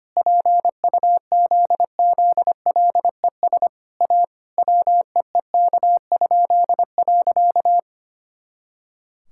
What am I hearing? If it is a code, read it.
Morse code, heard as a keyed tone, about 25 wpm, 700 Hz. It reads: PUZZLES A WEEK?.
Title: Pangram Challenge v2 - 25wpm